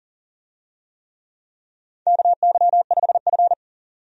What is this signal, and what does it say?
Morse code, heard as a keyed tone, about 42 wpm, 690 Hz. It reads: KY5F
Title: USA Call Signs - 4 Characters - 40wpm